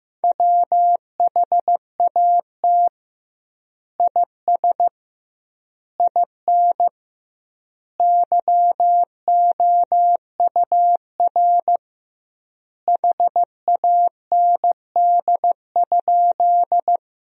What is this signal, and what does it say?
Morse code, heard as a keyed tone, about 15 wpm, 700 Hz. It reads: WHAT IS IN YOUR HAND?
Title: 2x Word-Spacing - Sentences from Top 200 Words - 15wpm